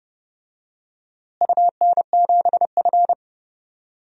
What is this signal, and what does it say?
Morse code, heard as a keyed tone, about 30 wpm, 700 Hz. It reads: UN7F
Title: DXCC-100 Call Signs - 4 Characters - Repeated 2x - 30wpm